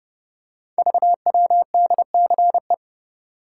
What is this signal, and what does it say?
Morse code, heard as a keyed tone, about 30 wpm, 700 Hz. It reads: VWDCE
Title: Code Groups - 5 Letters - 30wpm